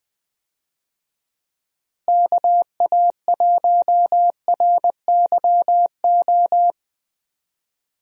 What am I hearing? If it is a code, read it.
Morse code, heard as a keyed tone, about 20 wpm, 700 Hz. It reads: KA1RYO
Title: USA Call Signs - 6 Characters - Repeated 3x - 20wpm